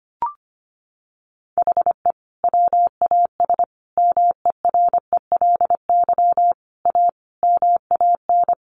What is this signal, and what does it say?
Morse code, heard as a keyed tone, about 25 wpm, 700 Hz. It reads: HE WAS MERELY A MAN
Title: Sentences from Top 1400 Words 25wpm